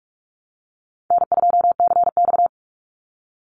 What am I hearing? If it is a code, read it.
Morse code, heard as a keyed tone, about 44 wpm, 700 Hz. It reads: N2CX
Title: Call Signs Repeated 2x - 45wpm